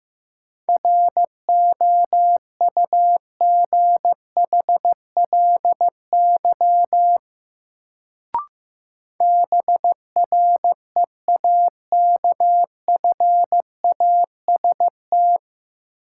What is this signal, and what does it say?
Morse code, heard as a keyed tone, about 15 wpm, 700 Hz. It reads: ROUGHLY BREAKFAST
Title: Sentences from Top 1500 Words - New Word Drill - 15wpm